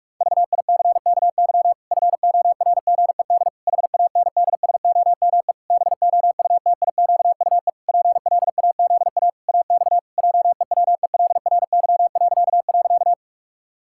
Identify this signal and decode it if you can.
Morse code, heard as a keyed tone, about 41 wpm, 700 Hz. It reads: VICKY FORGED HANDSOME BOUTIQUE PLAZA AX JEWELRY..
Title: Pangram Challenge v2 - 45wpm